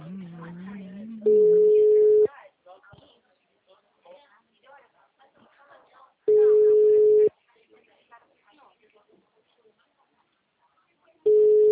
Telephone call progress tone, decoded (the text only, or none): ringback tone